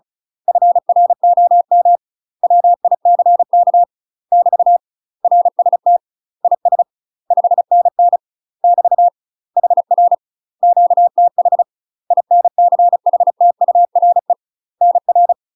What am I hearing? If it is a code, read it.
Morse code, heard as a keyed tone, about 34 wpm, 700 Hz. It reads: FROM WICK <BT> RST IS 5NN <BT> HR QTH INCHTURE NR
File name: UK MASTER QSOs TRAINING / 35WPM / UK MASTER QSOs 0004